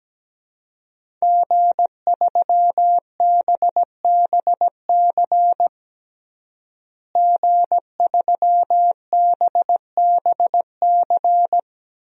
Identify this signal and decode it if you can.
Morse code, heard as a keyed tone, about 17 wpm, 700 Hz. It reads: G3BBC G3BBC
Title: DXCC-100 Call Signs - 5 Characters - Repeated 3x - 17wpm